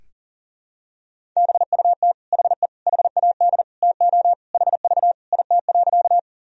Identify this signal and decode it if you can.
Morse code, heard as a keyed tone, about 40 wpm, 700 Hz. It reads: BUT HE HAD TO HV IT.
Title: Sentences from Top 100 Words and Abbreviations 40wpm